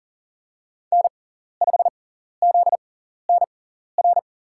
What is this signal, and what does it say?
Morse code, heard as a keyed tone, about 40 wpm, spaced as for 14 wpm, 700 Hz. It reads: N5ZNR
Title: USA Call Signs - 5 Characters - 6x Character Spacing - 40wpm